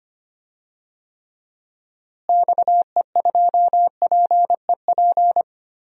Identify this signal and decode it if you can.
Morse code, heard as a keyed tone, about 25 wpm, 700 Hz. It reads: XE2PEP
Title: DXCC-100 Call Signs - 6 Characters - Repeated 2x - 25wpm